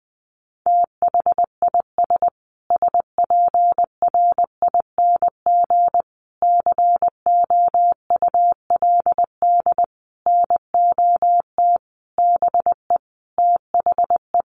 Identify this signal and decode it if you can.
Morse code, heard as a keyed tone, about 20 wpm, 700 Hz. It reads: THIS SPRING COULD NOT BE THE
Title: Sentences from Top 1500 Words 20wpm